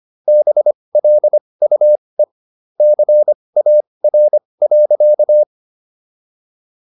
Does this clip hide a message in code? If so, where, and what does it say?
Yes: Morse code from 0.3 to 5.4 s, heard as a keyed tone, about 25 wpm, spaced as for 20 wpm, 600 Hz. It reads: BLUE CAR.